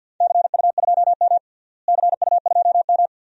Morse code, heard as a keyed tone, about 51 wpm, 700 Hz. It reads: CU2M CU2M
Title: DXCC-100 Call Signs - 4 Characters - Repeated 2x - 50wpm